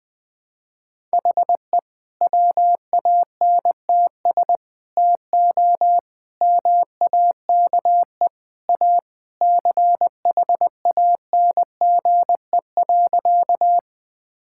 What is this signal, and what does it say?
Morse code, heard as a keyed tone, about 20 wpm, 700 Hz. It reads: HE WANTS TO MAKE A CHANGE.
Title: Sentences with 3-Word Phrases - Spoken Beginning - 20wpm